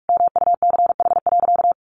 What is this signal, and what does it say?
Morse code, heard as a keyed tone, about 44 wpm, 700 Hz. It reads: MUCH.
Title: Sentences from Top 600 Words 45wpm